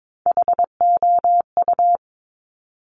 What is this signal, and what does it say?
Morse code, heard as a keyed tone, about 22 wpm, 700 Hz. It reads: HOU
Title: Top Three Letter Combinations in Words 22wpm